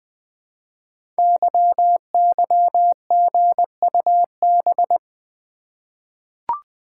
Morse code, heard as a keyed tone, about 20 wpm, 700 Hz. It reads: YYGUB
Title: Code Groups - 5 Letters - 20wpm